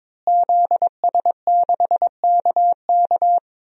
Morse code, heard as a keyed tone, about 21 wpm, 700 Hz. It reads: ZS6KK
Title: DXCC-100 Call Signs - 5 Characters - Repeated 3x - 22wpm